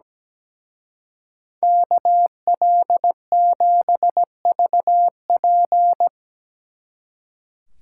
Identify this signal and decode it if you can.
Morse code, heard as a keyed tone, about 20 wpm, 700 Hz. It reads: KL7VP
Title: Speed Racing - USA 2x2 Call Signs 17wpm